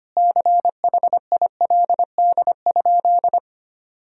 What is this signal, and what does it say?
Morse code, heard as a keyed tone, about 25 wpm, 700 Hz. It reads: CHILD?
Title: Sentences from Top 1000 Words - Review Part 1 - 25wpm